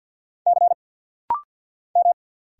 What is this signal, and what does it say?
Morse code, heard as a keyed tone, about 50 wpm, 700 Hz. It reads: C M
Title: Single Letter-Number Alt - Rapid-Fire 50wpm